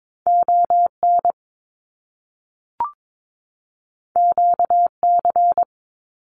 Morse code, heard as a keyed tone, about 22 wpm, 700 Hz. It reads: ON QC
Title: Canadian Provinces and Territories 22wpm